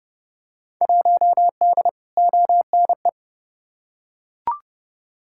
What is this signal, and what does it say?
Morse code, heard as a keyed tone, about 31 wpm, 700 Hz. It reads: 1D ONE
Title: Field Day Exchanges - 30wpm